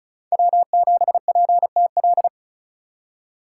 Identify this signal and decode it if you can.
Morse code, heard as a keyed tone, about 36 wpm, 700 Hz. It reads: W7PTL
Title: USA Call Signs - 5 Characters - 3x Character Spacing - 35wpm